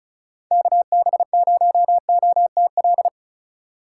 Morse code, heard as a keyed tone, about 36 wpm, 700 Hz. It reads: KB0OTL
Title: USA Call Signs - 6 Characters - 2x Character Spacing - 35wpm